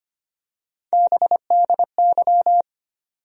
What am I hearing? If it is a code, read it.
Morse code, heard as a keyed tone, about 25 wpm, 700 Hz. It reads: BDY
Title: USA Call Sign Postfixes - 3 Letters 25wpm